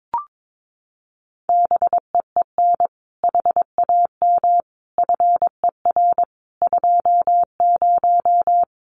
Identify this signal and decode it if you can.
Morse code, heard as a keyed tone, about 22 wpm, 700 Hz. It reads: BEEN HAM FER 20
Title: Common QSO Phrases 22wpm